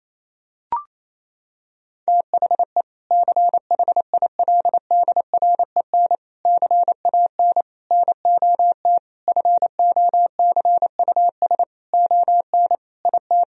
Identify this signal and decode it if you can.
Morse code, heard as a keyed tone, about 28 wpm, 700 Hz. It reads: THE CHILDREN CAN NOT FOCUS ON IT
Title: Sentences from Top 1000 Words - Review Part 4 - 28wpm